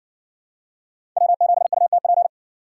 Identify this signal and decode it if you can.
Morse code, heard as a keyed tone, about 57 wpm, 700 Hz. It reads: W7UTP